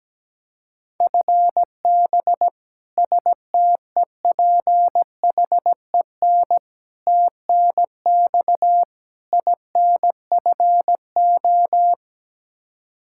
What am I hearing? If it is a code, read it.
Morse code, heard as a keyed tone, about 17 wpm, 700 Hz. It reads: FB STEPHEN TNX INFO
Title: Practice QSO - Exchange 3 - More Info - 17wpm